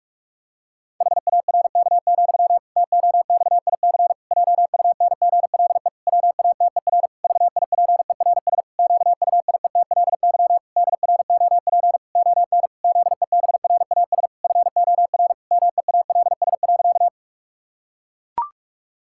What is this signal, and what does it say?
Morse code, heard as a keyed tone, about 41 wpm, 700 Hz. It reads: HARK, TOXIC JUNGLE WATER VIPERS QUIETLY DROP ON ZEBRAS FOR MEALS.
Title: Pangram Challenge v2 - 45wpm